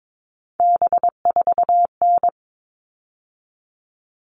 Morse code, heard as a keyed tone, about 22 wpm, 700 Hz. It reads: B4N
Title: All Cut Numbers - 3-Digit 22wpm